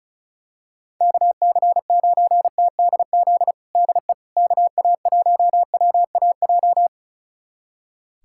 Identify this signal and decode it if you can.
Morse code, heard as a keyed tone, about 36 wpm, 690 Hz. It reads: KC9TDZ DE KA1WAJ
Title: USA Callsign DE Callsign - 6 Characters - 35wpm